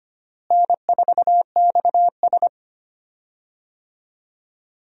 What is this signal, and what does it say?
Morse code, heard as a keyed tone, about 25 wpm, 700 Hz. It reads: N4XS
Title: USA Call Signs - 4 Characters - Repeated 2x - 25wpm